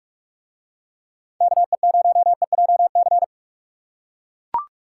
Morse code, heard as a keyed tone, about 41 wpm, 700 Hz. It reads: KE0EJC